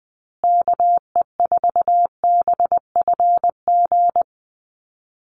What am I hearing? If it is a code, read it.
Morse code, heard as a keyed tone, about 20 wpm, 700 Hz. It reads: KE4BFG
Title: USA Call Signs Repeated 3x 20wpm